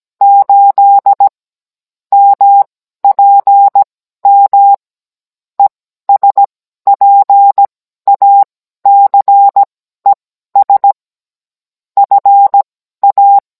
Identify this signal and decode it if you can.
Morse code, heard as a keyed tone, about 17 wpm, spaced as for 12 wpm, 800 Hz. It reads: 8 MPM ESPACES FA
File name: Cours-NR18-16mm